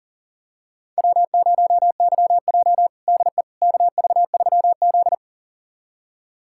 Morse code, heard as a keyed tone, about 37 wpm, 700 Hz. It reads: W0YJ DE KV3Z
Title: USA Callsign DE Callsign - 4 Characters - 40wpm